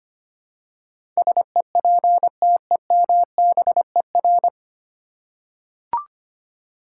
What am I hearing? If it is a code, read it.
Morse code, heard as a keyed tone, about 25 wpm, 700 Hz. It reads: SEPTEMBER
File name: Nine Letter Words - 25wpm